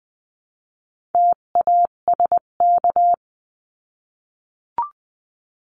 Morse code, heard as a keyed tone, about 20 wpm, 700 Hz. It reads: TASK
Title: 258 Lesson 42c - Words 16-20wpm